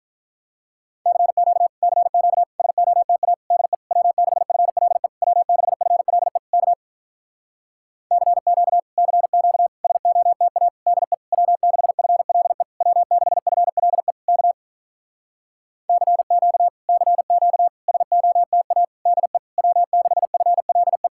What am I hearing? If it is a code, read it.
Morse code, heard as a keyed tone, about 47 wpm, 700 Hz. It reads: CQ CQ SOTA DE W6FLE W6FLE K CQ CQ SOTA DE W6FLE W6FLE K CQ CQ SOTA DE W6FLE